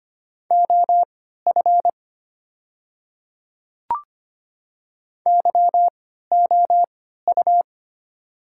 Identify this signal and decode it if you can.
Morse code, heard as a keyed tone, about 25 wpm, spaced as for 14 wpm, 700 Hz. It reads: OF YOU
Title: 3x Character Spacing - Top 100 Words - 25wpm